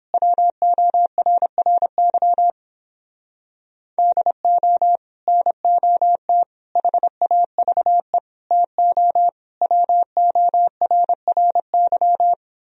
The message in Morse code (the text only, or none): WORRY DO NOT HAVE TO WORRY